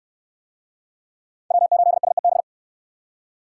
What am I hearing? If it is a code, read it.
Morse code, heard as a keyed tone, about 65 wpm, 700 Hz. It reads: W8SEB